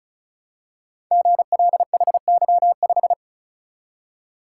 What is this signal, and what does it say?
Morse code, heard as a keyed tone, about 36 wpm, 700 Hz. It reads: GLHY5
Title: Code Groups - 5 Letter:Numbers - 35wpm